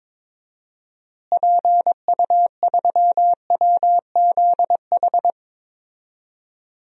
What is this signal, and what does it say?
Morse code, heard as a keyed tone, about 21 wpm, 700 Hz. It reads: PU3WZH
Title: DXCC-100 Call Signs - 6 Characters - Repeated 2x - 22wpm